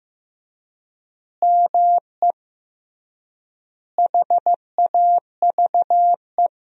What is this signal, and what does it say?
Morse code, heard as a keyed tone, about 15 wpm, 700 Hz. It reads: ME HAVE
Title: 3x Word-Spacing - Sentences from Top 300 Words - 15wpm